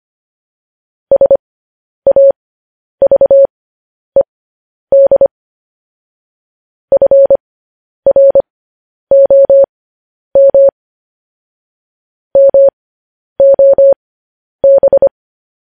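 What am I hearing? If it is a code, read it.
Morse code, heard as a keyed tone, about 25 wpm, spaced as for 10 wpm, 550 Hz. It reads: SAVED FROM MOB
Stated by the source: Morse code news headlines practice, at 10 WPM.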